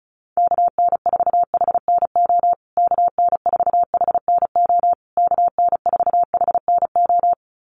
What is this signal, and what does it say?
Morse code, heard as a keyed tone, about 34 wpm, 700 Hz. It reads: KN4HNO KN4HNO KN4HNO
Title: USA Call Signs Repeated 3x 35wpm